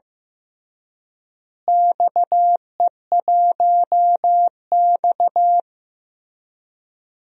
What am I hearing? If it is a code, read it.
Morse code, heard as a keyed tone, about 15 wpm, 700 Hz. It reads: XE1X
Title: DXCC-100 Call Signs - 4 Characters - Repeated 3x - 15wpm